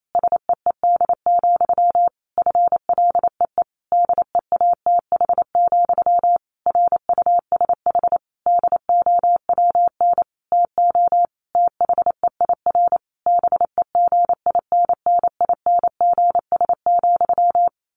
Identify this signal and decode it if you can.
Morse code, heard as a keyed tone, about 28 wpm, 700 Hz. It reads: SEED, FLEE DEATH, RUSH DOWN TO THEIR BEGINNINGS,
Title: A Sound Of Thunder 28wpm